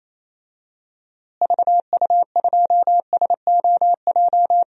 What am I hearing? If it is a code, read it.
Morse code, heard as a keyed tone, about 28 wpm, 700 Hz. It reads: VU2SOJ
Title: DXCC-100 Call Signs - 6 Characters - Repeated 2x - 28wpm